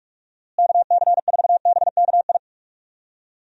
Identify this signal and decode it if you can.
Morse code, heard as a keyed tone, about 42 wpm, 700 Hz. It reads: KC4BKI
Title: USA Call Signs - 6 Characters - Repeated 3x - 45wpm